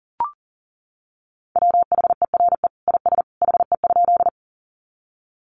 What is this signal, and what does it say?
Morse code, heard as a keyed tone, about 40 wpm, 700 Hz. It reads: WHERE IS HE?
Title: Sentences from Top 200 Words 40wpm